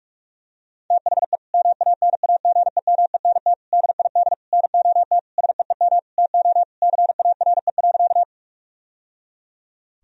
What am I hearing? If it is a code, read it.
Morse code, heard as a keyed tone, about 41 wpm, 700 Hz. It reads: THE MANAGEMENT DID NOT SEEM TO CARE.
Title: Sentences from Top 1000 Words - Review Part 4 - 45wpm